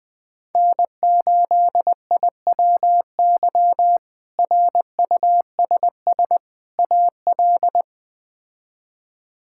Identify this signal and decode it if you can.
Morse code, heard as a keyed tone, about 20 wpm, 700 Hz. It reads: N8IWY RUSS AL
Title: K1USN SST QSO Practice - 20wpm